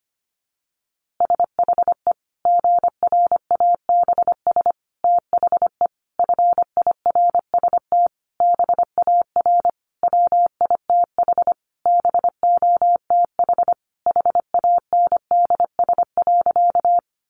Morse code, heard as a keyed tone, about 25 wpm, 700 Hz. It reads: SHE GRABS THE FIRST BAR WITH BOTH HANDS.
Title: Easy Phrases 25wpm